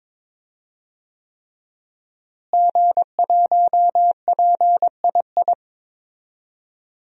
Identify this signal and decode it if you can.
Morse code, heard as a keyed tone, about 22 wpm, 700 Hz. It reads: G1PII
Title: DXCC-100 Call Signs - 5 Characters - Repeated 3x - 22wpm